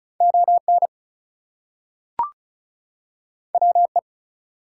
Morse code, heard as a keyed tone, about 36 wpm, 700 Hz. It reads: ON WE
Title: Two Letter Words - 35wpm